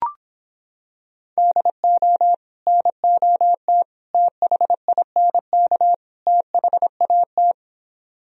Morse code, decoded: DO NOT THINK THAT

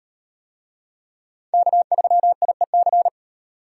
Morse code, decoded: K3IEC

